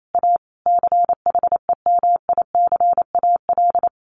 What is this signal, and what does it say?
Morse code, heard as a keyed tone, about 28 wpm, 700 Hz. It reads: A CHEMICAL
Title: A Sound Of Thunder 28wpm